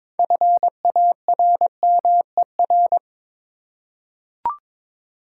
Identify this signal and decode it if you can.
Morse code, heard as a keyed tone, about 22 wpm, 700 Hz. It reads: FARMER
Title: Six Letter Words - 22wpm